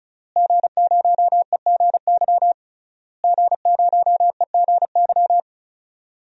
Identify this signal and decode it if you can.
Morse code, heard as a keyed tone, about 34 wpm, 700 Hz. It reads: G0EGY G0EGY
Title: DXCC-100 Call Signs - 5 Characters - Repeated 3x - 35wpm